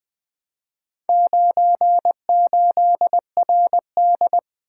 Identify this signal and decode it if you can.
Morse code, heard as a keyed tone, about 24 wpm, 700 Hz. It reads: 98RD
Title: Speed-Racing - Four Letter-Number Combinations - 20wpm